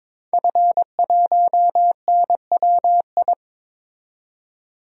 Morse code, heard as a keyed tone, about 22 wpm, 700 Hz. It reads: F1NWI
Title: DXCC-100 Call Signs - 5 Characters - 22wpm